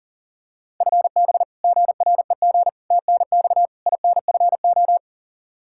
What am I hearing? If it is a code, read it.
Morse code, heard as a keyed tone, about 40 wpm, 700 Hz. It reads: FB GREG TNX INFO
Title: Practice QSO - Exchange 3 - More Info - 40wpm